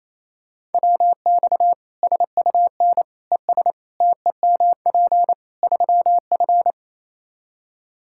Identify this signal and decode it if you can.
Morse code, heard as a keyed tone, about 28 wpm, 700 Hz. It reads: WX SUN ES TEMP 3F
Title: Practice QSO - Exchange 2 - Info - 28wpm